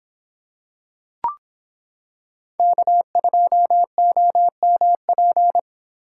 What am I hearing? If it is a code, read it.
Morse code, heard as a keyed tone, about 26 wpm, 700 Hz. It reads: K2OMP